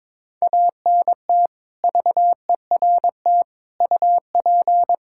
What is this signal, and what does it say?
Morse code, heard as a keyed tone, about 22 wpm, 700 Hz. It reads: ANT VERT UP
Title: Practice QSO - Exchange 2 - Info - 22wpm